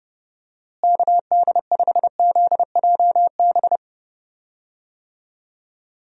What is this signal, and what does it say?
Morse code, heard as a keyed tone, about 32 wpm, 700 Hz. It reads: KD5ZJB
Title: USA Call Signs - 6 Characters - Repeated 3x - 30wpm